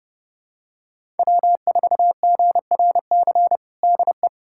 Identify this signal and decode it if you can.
Morse code, heard as a keyed tone, about 32 wpm, 700 Hz. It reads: W4GRC DE
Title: USA Callsign DE Callsign - 5 Characters - 30wpm